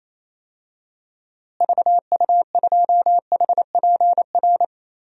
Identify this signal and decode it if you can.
Morse code, heard as a keyed tone, about 28 wpm, 700 Hz. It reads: VU2HPR
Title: DXCC-100 Call Signs - 6 Characters - Repeated 2x - 28wpm